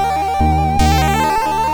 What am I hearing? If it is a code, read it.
There is no code.